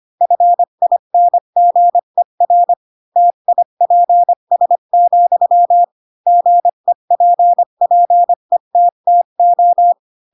Morse code, heard as a keyed tone, about 25 wpm, 700 Hz. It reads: FINGER TIPS, GEPPETTO
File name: Pinocchio / Pinocchio ch3 20 25 ik1hge